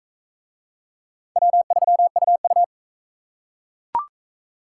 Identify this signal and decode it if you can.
Morse code, heard as a keyed tone, about 40 wpm, 700 Hz. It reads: W3UU